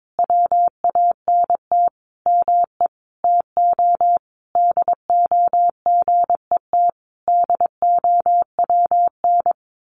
Morse code, heard as a keyed tone, about 22 wpm, 700 Hz. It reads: WANT ME TO DO GET DOWN
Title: A Sound Of Thunder 22wpm